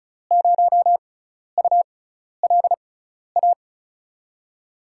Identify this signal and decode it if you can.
Morse code, heard as a keyed tone, about 36 wpm, spaced as for 12 wpm, 700 Hz. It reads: 0ULA
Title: USA Call Signs - 6 Characters - 6x Character Spacing - 35wpm